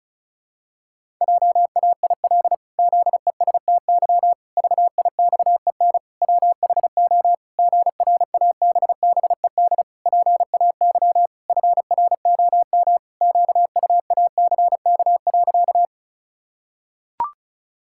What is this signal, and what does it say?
Morse code, heard as a keyed tone, about 36 wpm, 700 Hz. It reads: JAIL ZESTY VIXEN WHO GRABBED PAY FROM QUACK.
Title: Pangram Challenge v2 - 35wpm